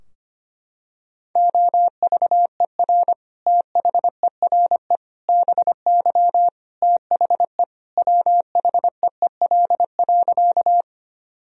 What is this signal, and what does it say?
Morse code, heard as a keyed tone, about 25 wpm, 700 Hz. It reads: OVER THERE BY THE WHEEL.
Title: Sentences from Top 1000 Words - Review Part 3 - 25wpm